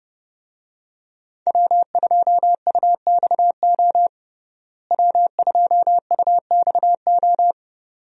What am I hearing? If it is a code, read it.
Morse code, heard as a keyed tone, about 30 wpm, 700 Hz. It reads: W2UXO W2UXO
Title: USA Call Signs - 5 Characters - Repeated 2x - 30wpm